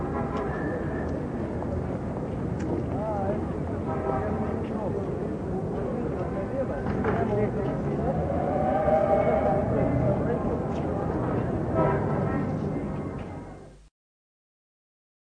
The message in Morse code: none